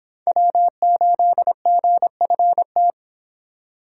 Morse code, heard as a keyed tone, about 30 wpm, 700 Hz. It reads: W8GFT